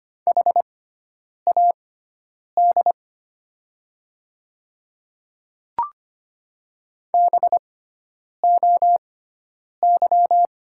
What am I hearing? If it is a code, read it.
Morse code, heard as a keyed tone, about 25 wpm, spaced as for 9 wpm, 700 Hz. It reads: HAD BOY
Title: 6x Character Spacing - Top 200 Words - 25wpm